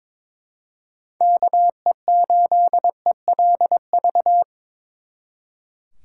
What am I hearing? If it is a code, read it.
Morse code, heard as a keyed tone, about 25 wpm, 700 Hz. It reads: KE8ELV
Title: Speed Racing - USA 2x3 Call Signs 22wpm